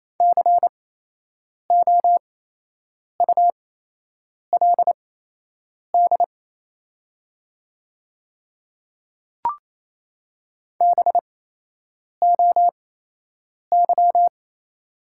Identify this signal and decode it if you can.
Morse code, heard as a keyed tone, about 28 wpm, spaced as for 8 wpm, 700 Hz. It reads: COULD BOY
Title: 8x Character Spacing - Top 300 Words - 28wpm